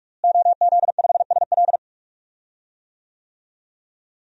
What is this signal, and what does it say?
Morse code, heard as a keyed tone, about 46 wpm, 700 Hz. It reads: OZ5SL
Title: DXCC-100 Call Signs - 5 Characters - Repeated 2x - 45wpm